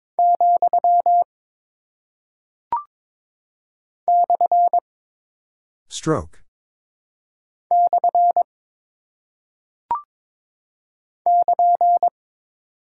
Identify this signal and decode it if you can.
Morse code, heard as a keyed tone, about 22 wpm, 700 Hz. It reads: , / / <KN>
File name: practice / 22wpm / Single Punctuation-Prosign - 22wpm